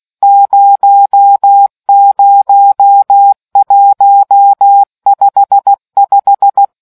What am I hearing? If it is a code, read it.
Morse code, heard as a keyed tone, about 16 wpm, 800 Hz. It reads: 00155